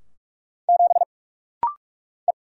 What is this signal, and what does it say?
Morse code, heard as a keyed tone, about 44 wpm, 700 Hz. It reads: 7 E
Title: Single Letter-Number Alt - Rapid-Fire 45wpm